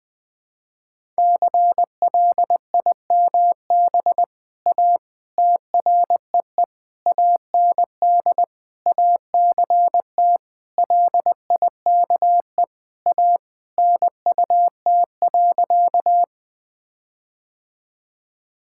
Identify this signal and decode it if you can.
Morse code, heard as a keyed tone, about 20 wpm, 700 Hz. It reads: CLIMB A TREE AND ACT LIKE A NUT.
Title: All-Age Appropriate Jokes - 20wpm